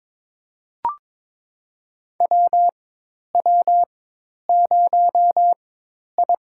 Morse code, heard as a keyed tone, about 22 wpm, spaced as for 10 wpm, 700 Hz. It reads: WW0I
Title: USA Call Signs - 4 Characters - 4x Character Spacing - 22wpm